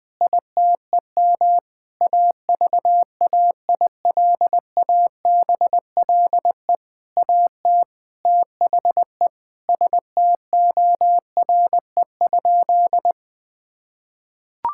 Morse code, heard as a keyed tone, about 21 wpm, 700 Hz. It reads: ITEM AVAILABLE AT THE STORE?
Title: Sentences from Top 1000 Words - Review Part 6 - 20wpm